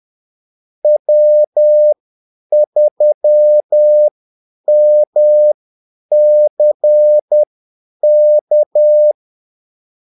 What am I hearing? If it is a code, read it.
Morse code, heard as a keyed tone, about 10 wpm, spaced as for 8 wpm, 600 Hz. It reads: W3MCK
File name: Callsigns-10-8wpm